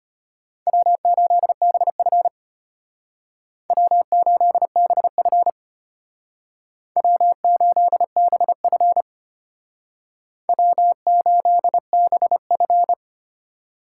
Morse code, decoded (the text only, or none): W8BF W8BF W8BF W8BF